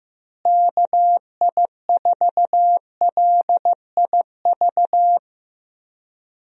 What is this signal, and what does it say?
Morse code, heard as a keyed tone, about 15 wpm, 700 Hz. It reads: KI4LIV
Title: USA Call Signs - 6 Characters - Repeated 3x - 15wpm